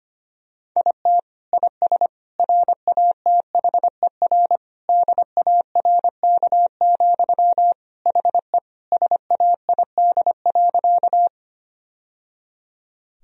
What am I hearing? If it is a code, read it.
Morse code, heard as a keyed tone, about 25 wpm, 700 Hz. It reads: IT IS RATHER DARK, HE SAID.
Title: Sentences from Top 1000 Words - Review Part 4 - 25wpm